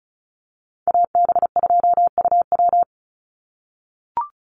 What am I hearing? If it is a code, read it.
Morse code, heard as a keyed tone, about 34 wpm, 700 Hz. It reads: AB2UW
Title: USA Call Signs 35wpm